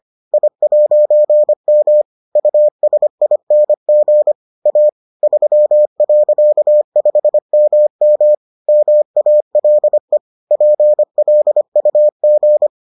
Morse code, heard as a keyed tone, about 25 wpm, 600 Hz. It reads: I'M USING A 3.5MM MALE PLUG